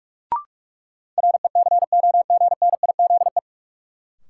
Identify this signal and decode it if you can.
Morse code, heard as a keyed tone, about 45 wpm, 700 Hz. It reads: RECOGNIZE
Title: Sentences from Top 1400 Words - New Word Drill - 45wpm